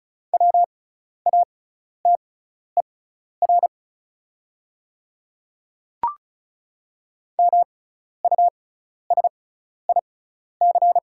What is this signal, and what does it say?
Morse code, heard as a keyed tone, about 36 wpm, spaced as for 12 wpm, 690 Hz. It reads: WATER MUSIC
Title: 6x Character Spacing - Top 500 Words - 35wpm